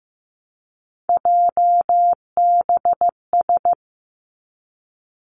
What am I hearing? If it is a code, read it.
Morse code, heard as a keyed tone, about 15 wpm, 700 Hz. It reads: JBS
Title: Three Challenging Letter-Number Combinations 15wpm